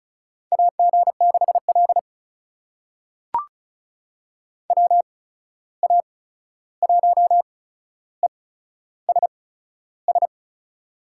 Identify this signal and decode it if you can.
Morse code, heard as a keyed tone, about 36 wpm, spaced as for 13 wpm, 700 Hz. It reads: AG6L WA1ESS
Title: USA Call Signs - 8x Character Spacing - 35wpm